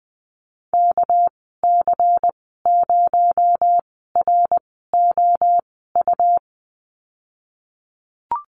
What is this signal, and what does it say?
Morse code, heard as a keyed tone, about 20 wpm, spaced as for 14 wpm, 700 Hz. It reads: KC0ROU
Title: 238 Lesson 38g - Letter q - Call Signs 20wpm